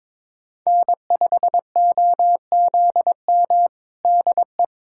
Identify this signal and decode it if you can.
Morse code, heard as a keyed tone, about 21 wpm, 700 Hz. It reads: N5OZM DE
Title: USA Callsign DE Callsign - 5 Characters - 22wpm